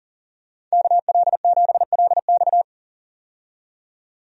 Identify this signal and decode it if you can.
Morse code, heard as a keyed tone, about 40 wpm, 700 Hz. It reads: KL7LX
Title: DXCC-100 Call Signs - 5 Characters - 40wpm